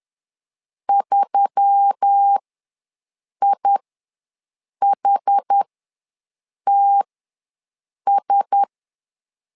Morse code, heard as a keyed tone, about 10 wpm, spaced as for 6 wpm, 800 Hz. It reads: 3IHTS
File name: LEZIONI / 4 Q1S3 ws 10wpm 2437